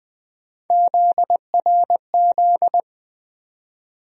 Morse code, heard as a keyed tone, about 21 wpm, 700 Hz. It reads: ZRZ
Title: USA Call Sign Postfixes - 3 Letters 20wpm